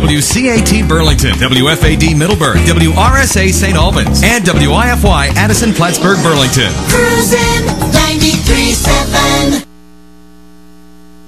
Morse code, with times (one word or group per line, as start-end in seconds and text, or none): none